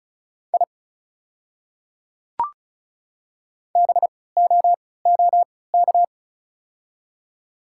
Morse code, heard as a keyed tone, about 34 wpm, spaced as for 20 wpm, 690 Hz. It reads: I BOOK
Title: 3x Character Spacing - Top 300 Words - 35wpm